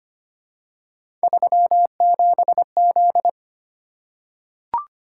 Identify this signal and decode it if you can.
Morse code, heard as a keyed tone, about 25 wpm, 700 Hz. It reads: 37Z
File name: Three Letter-Number Combinations - 25wpm